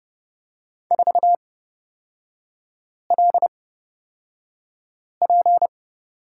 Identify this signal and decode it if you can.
Morse code, heard as a keyed tone, about 30 wpm, spaced as for 5 wpm, 700 Hz. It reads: 4LP
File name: Lesson 36e - BT - Three Characters 30wpm